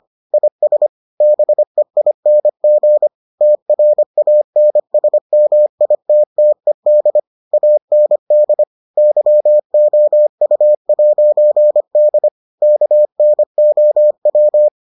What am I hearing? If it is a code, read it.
Morse code, heard as a keyed tone, about 25 wpm, 600 Hz. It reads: IS BEING TRANSMITTED AND YOU'D KNOW